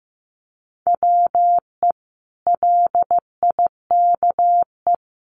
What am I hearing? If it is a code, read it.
Morse code, heard as a keyed tone, about 15 wpm, 700 Hz. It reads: WE LIKE
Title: Sentences from Top 1100 Words 15wpm